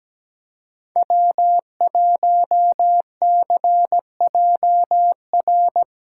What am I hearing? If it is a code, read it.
Morse code, heard as a keyed tone, about 17 wpm, 700 Hz. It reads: W1CJR
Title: USA Call Signs - 5 Characters - 3x Character Spacing - 17wpm